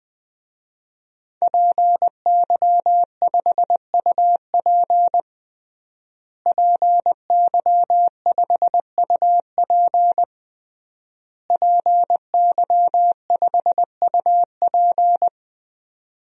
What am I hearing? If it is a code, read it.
Morse code, heard as a keyed tone, about 21 wpm, 700 Hz. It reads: PY5UP PY5UP PY5UP
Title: DXCC-100 Call Signs - 5 Characters - Repeated 3x - 20wpm